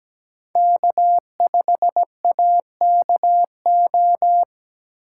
Morse code, heard as a keyed tone, about 17 wpm, 700 Hz. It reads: K5AKO
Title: USA Call Signs - 3x Character Spacing - 17wpm